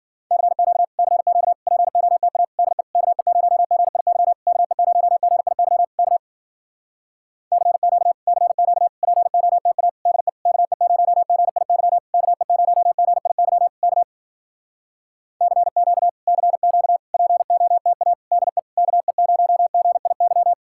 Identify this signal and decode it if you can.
Morse code, heard as a keyed tone, about 58 wpm, 700 Hz. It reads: CQ CQ POTA DE KE0GIY KE0GIY K CQ CQ POTA DE KE0GIY KE0GIY K CQ CQ POTA DE KE0GIY